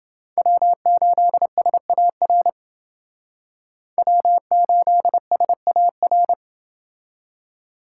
Morse code, heard as a keyed tone, about 28 wpm, 700 Hz. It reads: W8SAR W8SAR